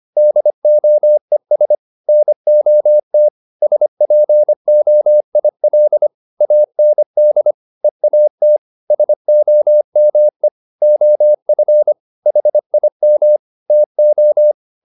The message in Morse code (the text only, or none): DOES NOT SPOIL AND EAT SOME OF HIM TO